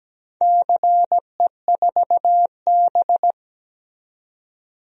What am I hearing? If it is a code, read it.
Morse code, heard as a keyed tone, about 17 wpm, 700 Hz. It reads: CE4B
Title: DXCC-100 Call Signs - 4 Characters - 17wpm